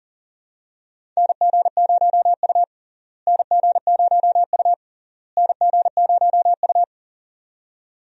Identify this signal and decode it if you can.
Morse code, heard as a keyed tone, about 40 wpm, 700 Hz. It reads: NG0U NG0U NG0U
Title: USA Call Signs - 4 Characters - Repeated 3x - 40wpm